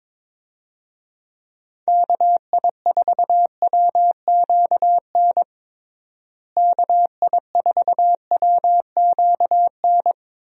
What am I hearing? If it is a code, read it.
Morse code, heard as a keyed tone, about 22 wpm, 700 Hz. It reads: KI4WQN KI4WQN
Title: USA Call Signs - 6 Characters - Repeated 2x - 22wpm